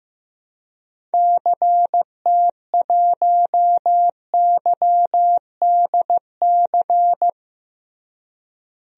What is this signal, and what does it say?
Morse code, heard as a keyed tone, about 15 wpm, 700 Hz. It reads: CT1YDC
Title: DXCC-100 Call Signs - 6 Characters - Repeated 3x - 15wpm